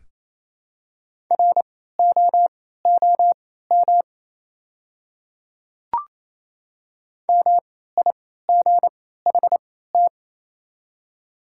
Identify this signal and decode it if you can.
Morse code, heard as a keyed tone, about 28 wpm, spaced as for 16 wpm, 700 Hz. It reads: ROOM MIGHT
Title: 3x Character Spacing - Top 300 Words - 28wpm